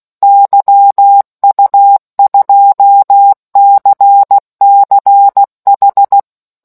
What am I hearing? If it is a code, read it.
Morse code, heard as a keyed tone, about 16 wpm, 800 Hz. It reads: YU2CCH